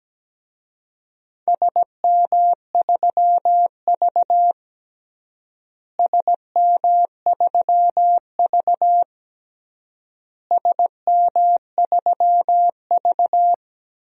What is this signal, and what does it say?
Morse code, heard as a keyed tone, about 17 wpm, 700 Hz. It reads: SM3V SM3V SM3V
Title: DXCC-100 Call Signs - 4 Characters - Repeated 3x - 17wpm